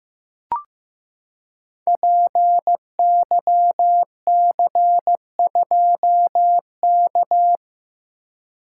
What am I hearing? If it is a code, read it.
Morse code, heard as a keyed tone, about 15 wpm, 700 Hz. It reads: PYC2K
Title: Code Groups - 5 Letter:Numbers - 15wpm